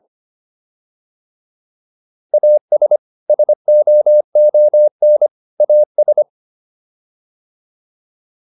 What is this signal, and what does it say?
Morse code, heard as a keyed tone, about 25 wpm, 600 Hz. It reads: AS SOON AS